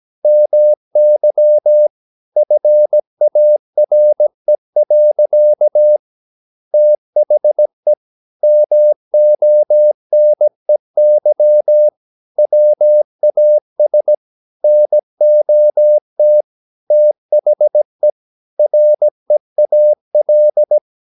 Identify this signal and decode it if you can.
Morse code, heard as a keyed tone, about 17 wpm, 600 Hz. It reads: MY FARE. THE MONEY WAS NOT THE REAL
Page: Romanen Never af Ken Follet på engelsk.